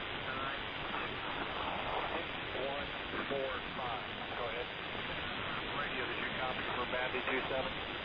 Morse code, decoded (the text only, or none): none